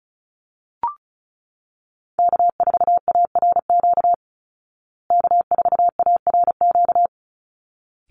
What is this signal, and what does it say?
Morse code, heard as a keyed tone, about 34 wpm, 700 Hz. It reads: K4ARQ K4ARQ
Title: Call Signs Repeated 3x - 35wpm